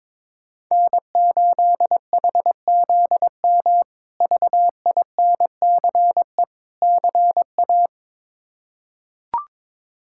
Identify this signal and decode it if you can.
Morse code, heard as a keyed tone, about 22 wpm, 700 Hz. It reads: N8HZM VINCE CA
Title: K1USN SST QSO Practice - 22wpm